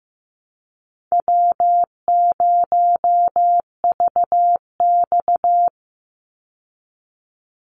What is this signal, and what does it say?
Morse code, heard as a keyed tone, about 15 wpm, 700 Hz. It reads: W0VX
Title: Call Signs Repeated 3x - 15wpm